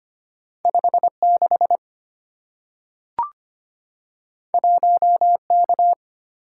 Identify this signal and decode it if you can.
Morse code, heard as a keyed tone, about 25 wpm, 700 Hz. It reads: 5 6 1 K
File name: Two Letter-Number Combinations - 25wpm